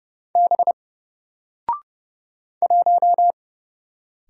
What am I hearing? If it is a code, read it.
Morse code, heard as a keyed tone, about 31 wpm, 700 Hz. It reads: B 1
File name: Single-Letter-Number-30wpm